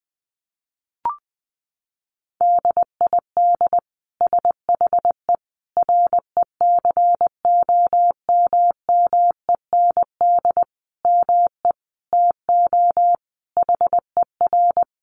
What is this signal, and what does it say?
Morse code, heard as a keyed tone, about 20 wpm, 700 Hz. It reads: DID SHE RECOMMEND ME TO HER
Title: Sentences from Top 1200 Words 20wpm